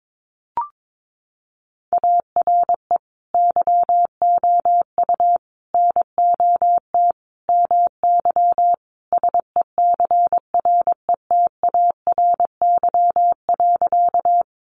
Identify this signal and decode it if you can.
Morse code, heard as a keyed tone, about 22 wpm, 700 Hz. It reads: ARE YOU NOT MY SECRETARY.
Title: Sentences from Top 1300 Words 22wpm